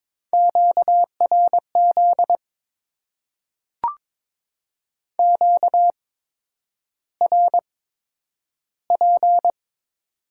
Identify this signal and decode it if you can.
Morse code, heard as a keyed tone, about 22 wpm, spaced as for 9 wpm, 700 Hz. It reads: QRZ QRP
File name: Top 90 QSO Elements - 8x Character Spacing - 22wpm